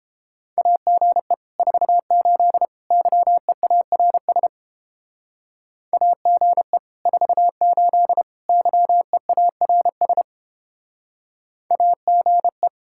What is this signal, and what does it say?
Morse code, heard as a keyed tone, about 29 wpm, 700 Hz. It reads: AGE 48 YEARS AGE 48 YEARS AGE